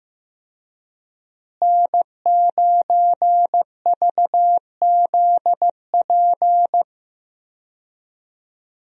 Morse code, heard as a keyed tone, about 15 wpm, 700 Hz. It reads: N9VZP
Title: USA Call Signs - 5 Characters - Repeated 2x - 15wpm